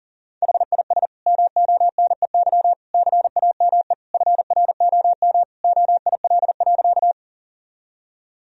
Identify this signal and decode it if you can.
Morse code, heard as a keyed tone, about 40 wpm, 700 Hz. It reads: HIS MONEY CAME FROM OIL.
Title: Sentences from Top 1000 Words - Review Part 4 - 40wpm